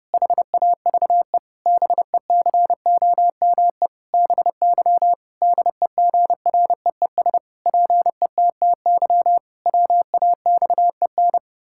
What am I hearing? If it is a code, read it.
Morse code, heard as a keyed tone, about 30 wpm, 700 Hz. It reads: HAVE BECOME BY DEGREES PETTY WAXEN
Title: Pangram Challenge v2 - 30wpm